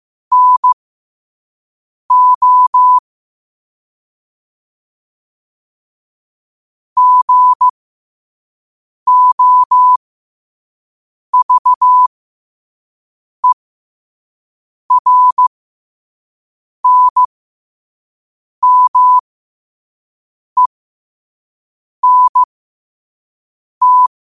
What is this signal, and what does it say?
Morse code, heard as a keyed tone, about 15 wpm, spaced as for 5 wpm, 1000 Hz. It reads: NO GOVERNMENT
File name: qotdfiles / 5wpm